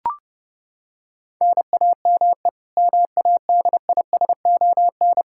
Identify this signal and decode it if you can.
Morse code, heard as a keyed tone, about 30 wpm, 700 Hz. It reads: NAME MADISON
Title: K1USN SST QSO Practice 30wpm